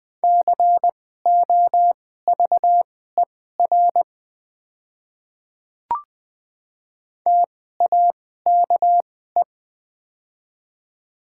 Morse code, heard as a keyed tone, about 20 wpm, spaced as for 14 wpm, 700 Hz. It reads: COVER TAKE
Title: 2x Character Spacing - Top 500 Words - 20wpm